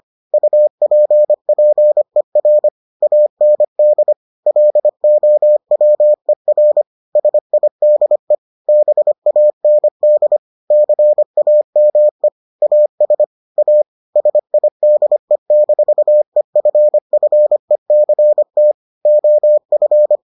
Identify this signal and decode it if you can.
Morse code, heard as a keyed tone, about 25 wpm, 600 Hz. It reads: UPPER AND LOWER SIDE BAND CAME AS A SIDE-EFFECT OF